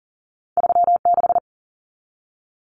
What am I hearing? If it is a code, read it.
Morse code, heard as a keyed tone, about 40 wpm, 700 Hz. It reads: 36
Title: Common Cut Numbers - 2-Digit 40wpm